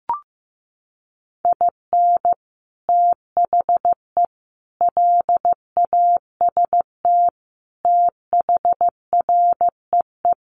Morse code, heard as a keyed tone, about 15 wpm, 700 Hz. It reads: IN THE LAST THREE
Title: Sets of 5 Words Encore2 15wpm